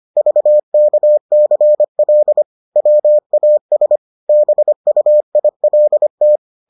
Morse code, heard as a keyed tone, about 25 wpm, 600 Hz. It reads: VKCL WAS BUILT